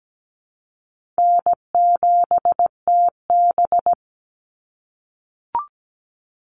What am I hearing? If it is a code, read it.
Morse code, heard as a keyed tone, about 17 wpm, 700 Hz. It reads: N7TB
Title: Call Signs - 17wpm